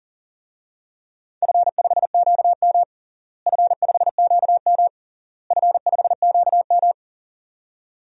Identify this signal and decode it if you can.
Morse code, heard as a keyed tone, about 40 wpm, 700 Hz. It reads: F5QM F5QM F5QM
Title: DXCC-100 Call Signs - 4 Characters - Repeated 3x - 40wpm